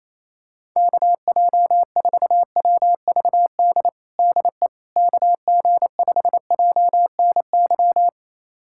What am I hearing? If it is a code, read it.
Morse code, heard as a keyed tone, about 28 wpm, 700 Hz. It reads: KJ4WVD DE KG5JNY
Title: USA Callsign DE Callsign - 6 Characters - 28wpm